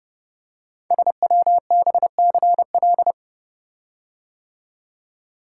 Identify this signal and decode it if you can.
Morse code, heard as a keyed tone, about 29 wpm, 700 Hz. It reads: SWBCL
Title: Code Groups - 5 Letters - 30wpm